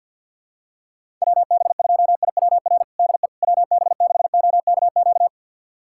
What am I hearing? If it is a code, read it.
Morse code, heard as a keyed tone, about 50 wpm, 700 Hz. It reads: WB2IWR DE WB6OKQ
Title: USA Callsign DE Callsign 50wpm